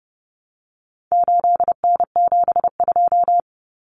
Morse code, heard as a keyed tone, about 30 wpm, 700 Hz. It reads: 8N72
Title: Common Cut Numbers - 4-Digit 30wpm